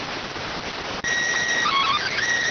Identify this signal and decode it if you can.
SSTV (slow-scan television), PD 290.